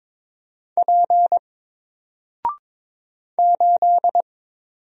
Morse code, heard as a keyed tone, about 22 wpm, 700 Hz. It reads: P 8
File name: Single-Letter-Number-22wpm